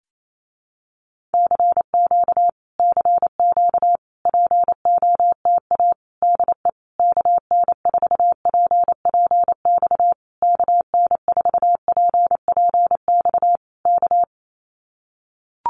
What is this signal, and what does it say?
Morse code, heard as a keyed tone, about 28 wpm, 700 Hz. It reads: CQ CQ POTA DE KN4PPX KN4PPX K
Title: Common QSO Phrases Encore 28wpm